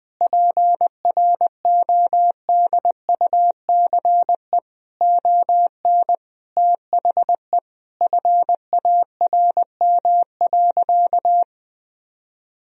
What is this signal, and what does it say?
Morse code, heard as a keyed tone, about 21 wpm, 700 Hz. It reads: PRODUCE ON THE FARM.
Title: Sentences from Top 1000 Words - Review Part 3 - 20wpm